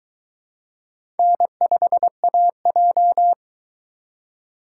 Morse code, heard as a keyed tone, about 22 wpm, 700 Hz. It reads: N5AJ